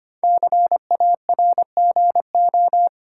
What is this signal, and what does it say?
Morse code, heard as a keyed tone, about 25 wpm, 700 Hz. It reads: CARGO
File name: 5 Letter Words - First 3 Letters Given - 25wpm